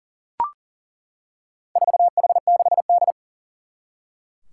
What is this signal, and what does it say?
Morse code, heard as a keyed tone, about 40 wpm, 700 Hz. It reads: 4H6D
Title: Four Challenging Letter-Number Combinations 40wpm